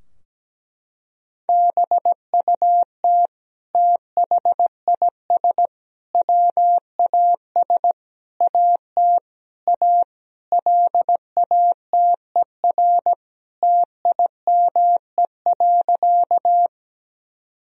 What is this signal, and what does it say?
Morse code, heard as a keyed tone, about 17 wpm, 700 Hz. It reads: BUT THIS WAS AT A LATER TIME.
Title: Sentences from Top 1000 Words - Review Part 4 - 17wpm